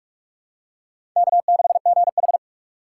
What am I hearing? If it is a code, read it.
Morse code, heard as a keyed tone, about 44 wpm, 690 Hz. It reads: K6GH
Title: USA Call Signs - 4 Characters - 8x Character Spacing - 45wpm